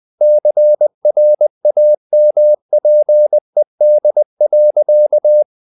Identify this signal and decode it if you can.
Morse code, heard as a keyed tone, about 20 wpm, 600 Hz. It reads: CRAMPED.
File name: the old man and the sea / the old man and the sea part 1 20wpm